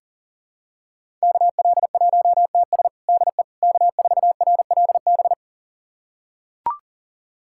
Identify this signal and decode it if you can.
Morse code, heard as a keyed tone, about 40 wpm, 700 Hz. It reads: KL1TS DE K4RLB
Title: USA Callsign DE Callsign 40wpm